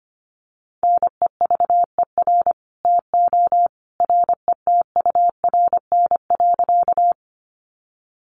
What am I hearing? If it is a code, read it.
Morse code, heard as a keyed tone, about 25 wpm, 700 Hz. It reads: NEVER TO RETURN.
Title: Sentences from Top 800 Words 25wpm